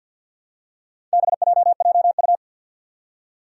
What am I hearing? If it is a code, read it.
Morse code, heard as a keyed tone, about 50 wpm, 700 Hz. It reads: BJJU
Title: Four Challenging Letter-Number Combinations 50wpm